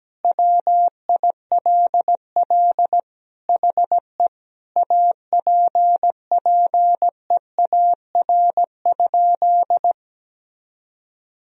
Morse code, heard as a keyed tone, about 17 wpm, 700 Hz. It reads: WILL HE APPEAR?
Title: Sentences from Top 1000 Words - Review Part 2 - 17wpm